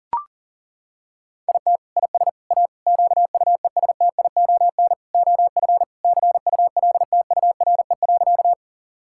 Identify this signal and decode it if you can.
Morse code, heard as a keyed tone, about 40 wpm, 700 Hz. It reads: IT IS A QUESTION OF CULTURE.
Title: Sentences from Top 1000 Words - Review Part 6 - 40wpm